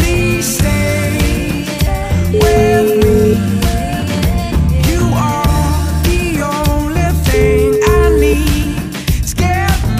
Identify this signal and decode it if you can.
Ringback tone (Europe, 425 Hz), with noise.